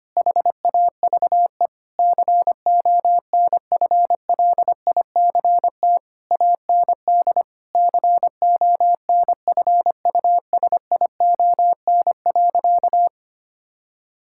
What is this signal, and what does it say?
Morse code, heard as a keyed tone, about 25 wpm, 700 Hz. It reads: HAVE CONFLICT AND CONFUSION.
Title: Sentences from Top 1600 Words 25wpm